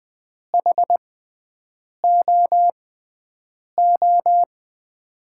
Morse code, heard as a keyed tone, about 21 wpm, spaced as for 7 wpm, 700 Hz. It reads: HOO
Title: USA Call Signs - 6 Characters - 6x Character Spacing - 20wpm